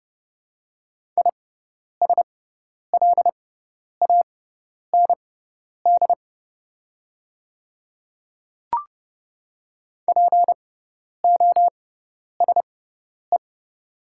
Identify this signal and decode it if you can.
Morse code, heard as a keyed tone, about 30 wpm, spaced as for 10 wpm, 700 Hz. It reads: ISLAND POSE
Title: 6x Character Spacing - Top 500 Words - 30wpm